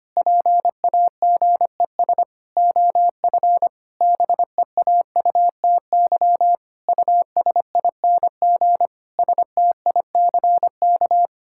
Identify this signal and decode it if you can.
Morse code, heard as a keyed tone, about 25 wpm, 700 Hz. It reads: PAGES OF BEAUTY USING STICK
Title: Pangram Challenge v2 - 25wpm